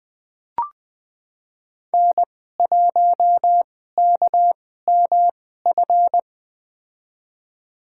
Morse code, heard as a keyed tone, about 20 wpm, spaced as for 14 wpm, 700 Hz. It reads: N1KMF
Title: USA Call Signs - 5 Characters - 2x Character Spacing - 20wpm